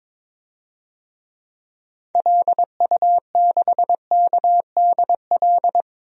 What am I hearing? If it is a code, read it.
Morse code, heard as a keyed tone, about 22 wpm, 700 Hz. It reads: LU6KDL
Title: DXCC-100 Call Signs - 6 Characters - Repeated 3x - 22wpm